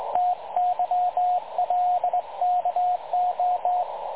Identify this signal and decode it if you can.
Morse code, heard as a keyed tone, about 21 wpm, 720 Hz, with radio noise. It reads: TYLKO
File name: dlazaawansowanych / nagraniazpasma / websdr recording 3565.0kHz